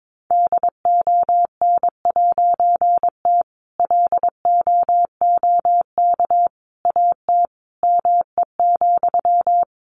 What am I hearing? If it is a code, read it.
Morse code, heard as a keyed tone, about 22 wpm, 700 Hz. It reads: DON'T LOOK AT ME,
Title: A Sound Of Thunder 22wpm